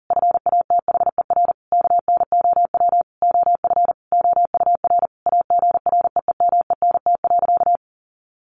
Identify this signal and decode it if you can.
Morse code, heard as a keyed tone, about 40 wpm, 700 Hz. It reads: FATHER KNOW OF OUR AGREEMENT.
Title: Sentences from Top 1500 Words 40wpm